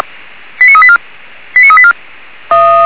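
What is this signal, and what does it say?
Five-tone selective call (ZVEI), heard as quick runs of five tones, tones of about 70 ms. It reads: 89384 89384
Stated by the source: ZVEI five-tone selective call received on an SDR.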